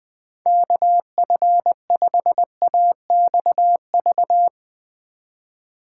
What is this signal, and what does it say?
Morse code, heard as a keyed tone, about 20 wpm, 700 Hz. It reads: KF5AXV
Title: USA Call Signs - 6 Characters - 4x Character Spacing - 20wpm